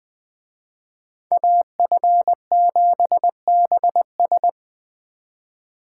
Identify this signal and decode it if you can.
Morse code, heard as a keyed tone, about 21 wpm, 700 Hz. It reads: AF7BS